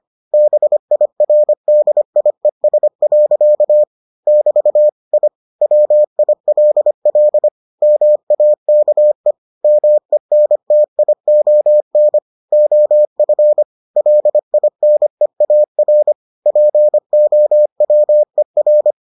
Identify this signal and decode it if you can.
Morse code, heard as a keyed tone, about 25 wpm, 600 Hz. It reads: BIRDIES. <BT> I WILL MAKE MENTION OF LINEAR POWER